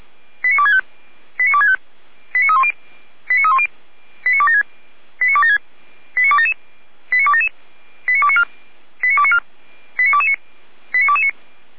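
Five-tone selective call (ZVEI), heard as quick runs of five tones, tones of about 70 ms. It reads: 89265 89265 89310 89276 89276 89277 89277 89294 89294 89229 89229
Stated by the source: ZVEI five-tone selective call received on an SDR, fire brigade (Feuerwehr) Orlamünde.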